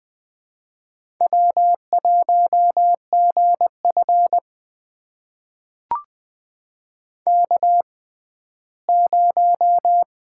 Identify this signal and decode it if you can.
Morse code, heard as a keyed tone, about 20 wpm, spaced as for 14 wpm, 700 Hz. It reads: W1GF K0
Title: USA Call Signs - 4 Characters - 6x Character Spacing - 20wpm